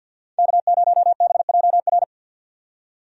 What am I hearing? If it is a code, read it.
Morse code, heard as a keyed tone, about 47 wpm, 700 Hz. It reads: K0BJR
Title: USA Call Signs - 5 Characters - 50wpm